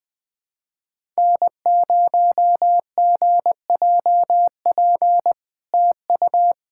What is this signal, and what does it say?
Morse code, heard as a keyed tone, about 21 wpm, 700 Hz. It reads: N0GJP TU
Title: SOTA QSO Practice 20wpm